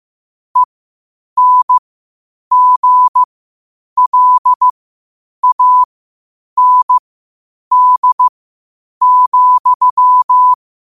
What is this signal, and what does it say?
Morse code, heard as a keyed tone, about 15 wpm, spaced as for 8 wpm, 1000 Hz. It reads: ENGLAND,
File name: 8wpm